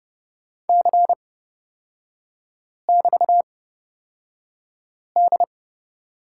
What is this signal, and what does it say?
Morse code, heard as a keyed tone, about 29 wpm, spaced as for 5 wpm, 700 Hz. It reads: C<BT>D
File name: Lesson 36e - BT - Three Characters 30wpm